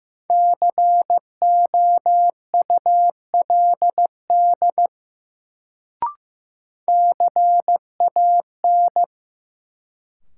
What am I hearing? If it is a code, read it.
Morse code, heard as a keyed tone, about 15 wpm, 700 Hz. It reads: COULD CAN
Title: Sentences from Top 100 Words - New Word Drill - 15wpm